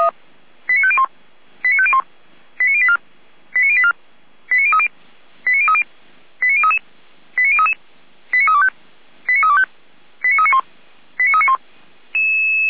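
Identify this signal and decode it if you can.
Five-tone selective call (ZVEI), heard as quick runs of five tones, tones of about 70 ms. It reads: 89501 89501 89084 89084 89030 89030 89033 89033 89325 89325 89391 89391
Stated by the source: Received on an SDR.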